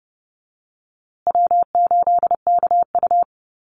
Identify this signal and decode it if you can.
Morse code, heard as a keyed tone, about 30 wpm, 700 Hz. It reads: W8KU
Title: USA Call Signs Repeated 2x 30wpm